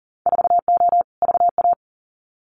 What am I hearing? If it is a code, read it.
Morse code, heard as a keyed tone, about 41 wpm, 700 Hz. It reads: 4O VA
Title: Winter Field Day Exchanges Encore 40wpm